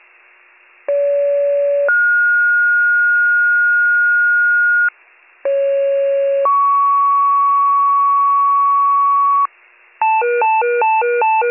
Two-tone page, heard as a one-second tone, then a three-second tone, about 570 then 1430 Hz, about 550 then 1120 Hz, with noise. It is heard from 0.9 to 9.5 s.